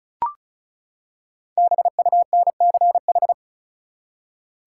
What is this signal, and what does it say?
Morse code, heard as a keyed tone, about 36 wpm, 700 Hz. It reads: BUNCH
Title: Five Letter Words - 35wpm